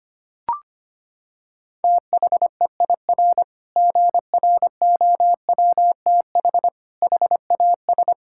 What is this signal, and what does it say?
Morse code, heard as a keyed tone, about 25 wpm, 700 Hz. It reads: THEIR GROWTH HAS
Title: Sentences from Top 1000 Words - Review Part 5 - 25wpm